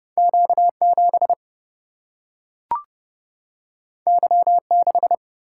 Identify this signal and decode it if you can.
Morse code, heard as a keyed tone, about 30 wpm, 700 Hz. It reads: Q 7 Y 6
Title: Single-Letter Single-Number 30wpm